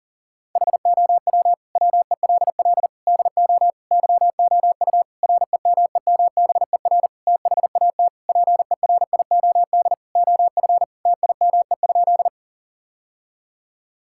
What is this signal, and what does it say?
Morse code, heard as a keyed tone, about 40 wpm, 700 Hz. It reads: HOW WELL DO YOU REMEMBER THAT PERIOD OF TIME?
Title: Sentences from Top 1000 Words - Review Part 5 - 40wpm